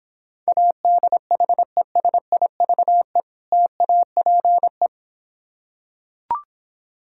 Morse code, heard as a keyed tone, about 26 wpm, 700 Hz. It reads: ADHESIVE TAPE